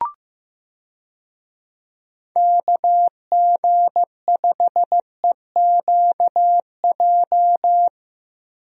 Morse code, heard as a keyed tone, about 15 wpm, 700 Hz. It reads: KG5EQJ
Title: USA Call Signs - 6 Characters - Repeated 3x - 15wpm